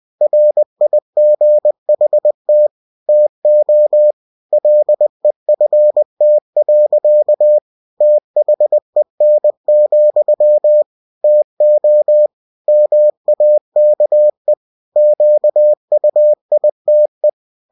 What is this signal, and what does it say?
Morse code, heard as a keyed tone, about 20 wpm, 600 Hz. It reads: RIGHT TO LEFT. THEN, TO MAKE QUITE
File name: cw-books / winnie-the-pooh / winnie-20wpm-page-0029